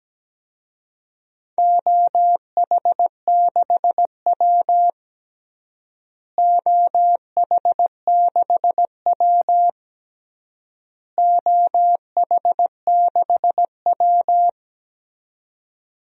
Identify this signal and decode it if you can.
Morse code, heard as a keyed tone, about 17 wpm, 700 Hz. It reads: OH6W OH6W OH6W
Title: DXCC-100 Call Signs - 4 Characters - Repeated 3x - 17wpm